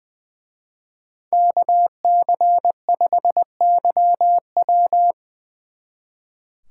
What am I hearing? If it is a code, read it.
Morse code, heard as a keyed tone, about 24 wpm, 700 Hz. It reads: KC5YW
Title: Speed Racing - USA Call Signs 20wpm